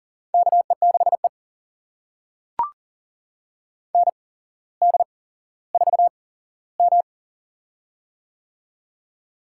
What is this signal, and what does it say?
Morse code, heard as a keyed tone, about 38 wpm, spaced as for 17 wpm, 700 Hz. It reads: KE6E ND4M
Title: USA Call Signs - 4 Characters - 8x Character Spacing - 40wpm